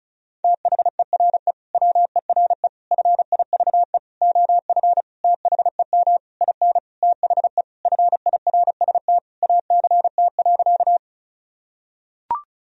Morse code, heard as a keyed tone, about 34 wpm, 700 Hz. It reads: THERE WERE FIVE OF THEM IN THE FIRST ACT.
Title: Sentences from Top 1000 Words - Review Part 2 - 35wpm